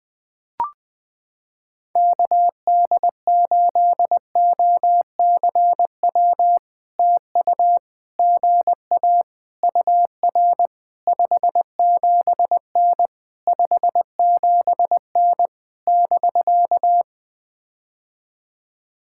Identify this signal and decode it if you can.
Morse code, heard as a keyed tone, about 20 wpm, 700 Hz. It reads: KD8OCW TU GA UR 57N 57N <BK>
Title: POTA QSO Practice - 20wpm